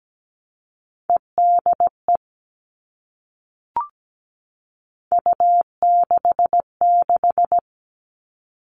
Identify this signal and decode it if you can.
Morse code, heard as a keyed tone, about 17 wpm, 700 Hz. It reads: EDE U66
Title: All Cut Numbers - 3-Digit 17wpm